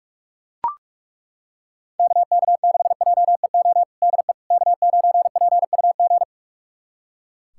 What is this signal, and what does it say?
Morse code, heard as a keyed tone, about 42 wpm, 700 Hz. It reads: KK6JEO DE K9PUG
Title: USA Callsign DE Callsign 45wpm